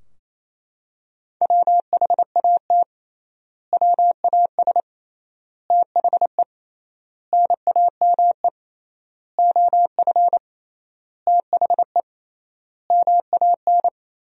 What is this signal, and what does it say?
Morse code, heard as a keyed tone, about 28 wpm, 700 Hz. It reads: WHAT WAS THE NAME OF THE MAN
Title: 3x Word-Spacing - Sentences from Top 300 Words - 28wpm